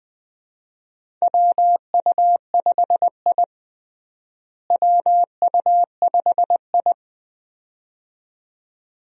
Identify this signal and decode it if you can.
Morse code, heard as a keyed tone, about 20 wpm, 700 Hz. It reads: WU5I WU5I
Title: USA Call Signs - 4 Characters - Repeated 2x - 20wpm